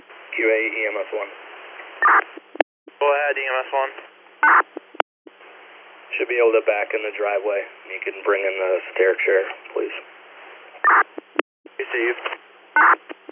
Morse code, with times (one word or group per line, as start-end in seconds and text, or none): none